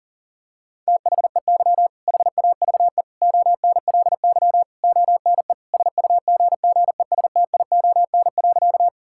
Morse code, heard as a keyed tone, about 40 wpm, 700 Hz. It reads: THEY HAVE ONLY ONE SUGGESTION.
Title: Sentences from Top 1600 Words 40wpm